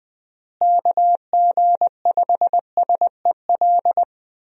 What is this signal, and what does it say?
Morse code, heard as a keyed tone, about 20 wpm, 700 Hz. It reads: KG5SEL
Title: USA Call Signs - 6 Characters - Repeated 3x - 20wpm